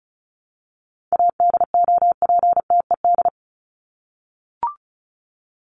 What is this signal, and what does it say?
Morse code, heard as a keyed tone, about 34 wpm, 700 Hz. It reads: ADOPTED
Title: ED Suffix - Top 100 Words with Base Words 35wpm